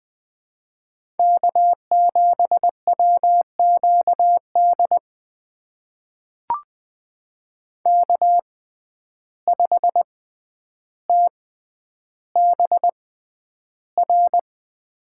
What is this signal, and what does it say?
Morse code, heard as a keyed tone, about 20 wpm, spaced as for 10 wpm, 700 Hz. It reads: K7WQD K5TBR
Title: USA Call Signs - 6x Character Spacing - 20wpm